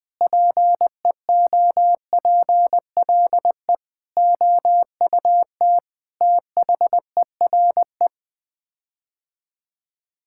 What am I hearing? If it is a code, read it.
Morse code, heard as a keyed tone, about 21 wpm, 700 Hz. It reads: PEOPLE OUT THERE
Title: Sets of 8 Words - 20wpm